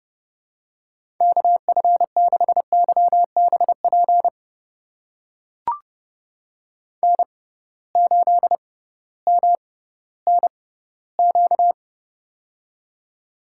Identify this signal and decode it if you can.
Morse code, heard as a keyed tone, about 32 wpm, spaced as for 17 wpm, 700 Hz. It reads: KF6YBP N8MNQ
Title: USA Call Signs - 6x Character Spacing - 30wpm